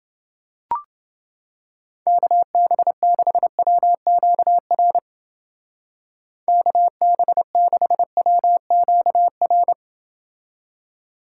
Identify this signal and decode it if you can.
Morse code, heard as a keyed tone, about 28 wpm, 700 Hz. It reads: KB6WQR KB6WQR